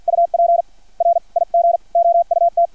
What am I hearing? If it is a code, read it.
Morse code, heard as a keyed tone, about 50 wpm, spaced as for 41 wpm, 680 Hz. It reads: UP RIG OUT
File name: archived / CRSnet